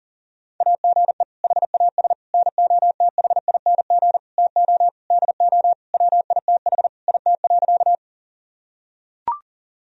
Morse code, heard as a keyed tone, about 37 wpm, 700 Hz. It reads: AGE HAS NOTHING TO DO WITH IT.
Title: Sentences from Top 1000 Words - Review Part 3 - 40wpm